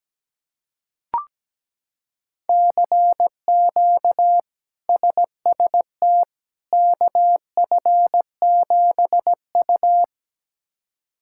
Morse code, heard as a keyed tone, about 17 wpm, 700 Hz. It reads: CQ SST KF7U
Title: K1USN SST QSO Practice - 17wpm